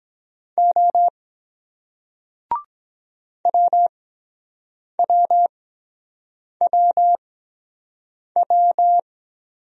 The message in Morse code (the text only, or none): O WWWW